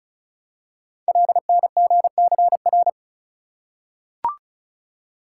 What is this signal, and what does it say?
Morse code, heard as a keyed tone, about 36 wpm, 700 Hz. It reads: LNGCR
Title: Code Groups - 5 Letters - 35wpm